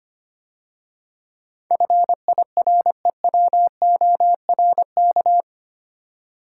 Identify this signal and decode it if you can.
Morse code, heard as a keyed tone, about 25 wpm, 700 Hz. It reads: FIREWORK
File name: 8 Letter Words - First 3 Letters Given - 25wpm